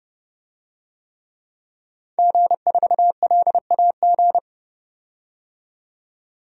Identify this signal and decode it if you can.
Morse code, heard as a keyed tone, about 30 wpm, 700 Hz. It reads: G4LAG
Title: DXCC-100 Call Signs - 5 Characters - 30wpm